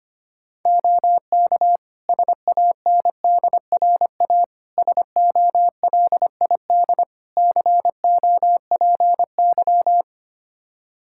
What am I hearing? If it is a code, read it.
Morse code, heard as a keyed tone, about 25 wpm, 700 Hz. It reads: OK SANDRA SOLID COPY
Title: Practice QSO - Exchange 3 - More Info - 25wpm